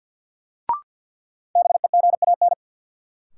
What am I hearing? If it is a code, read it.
Morse code, heard as a keyed tone, about 49 wpm, 700 Hz. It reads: BEGAN
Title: Sentences from Top 300 Words - New Word Drill - 50wpm